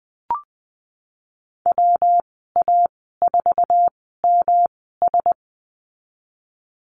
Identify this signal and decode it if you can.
Morse code, heard as a keyed tone, about 20 wpm, spaced as for 15 wpm, 700 Hz. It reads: WA4MS
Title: 207 Lesson 33g - Letter j - Call Signs 20wpm